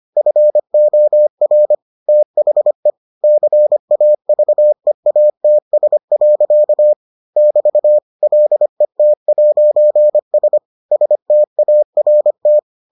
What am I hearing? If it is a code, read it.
Morse code, heard as a keyed tone, about 25 wpm, 600 Hz. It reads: FOR THE CAVEATS. <BT> LET'S START